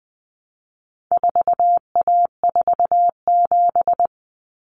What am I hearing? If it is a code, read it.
Morse code, heard as a keyed tone, about 20 wpm, 700 Hz. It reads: 4A47
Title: Common Cut Numbers - 4-Digit 20wpm